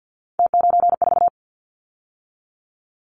Morse code, heard as a keyed tone, about 50 wpm, 700 Hz. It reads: T94
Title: Callsign Prefixes for Top 100 DXCC - 50wpm